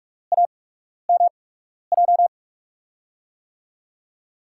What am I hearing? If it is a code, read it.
Morse code, heard as a keyed tone, about 44 wpm, spaced as for 12 wpm, 690 Hz. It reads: AMJ
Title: USA Call Signs - 5 Characters - 8x Character Spacing - 45wpm